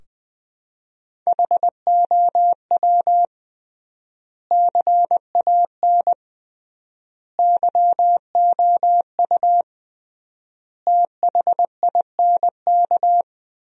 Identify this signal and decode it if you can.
Morse code, heard as a keyed tone, about 20 wpm, 700 Hz. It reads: HOW CAN YOU THINK
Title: 3x Word-Spacing - Sentences from Top 200 Words - 20wpm